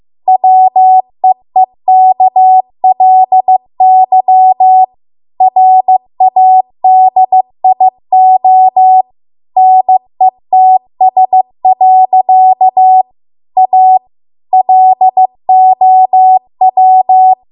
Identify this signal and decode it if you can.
Morse code, heard as a keyed tone, about 15 wpm, 750 Hz. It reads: WEEKLY RADIO NETS. A LOW